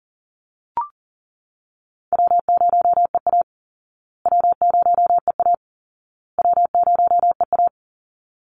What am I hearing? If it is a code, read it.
Morse code, heard as a keyed tone, about 40 wpm, 700 Hz. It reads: W0EA W0EA W0EA
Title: Call Signs Repeated 2x - 40wpm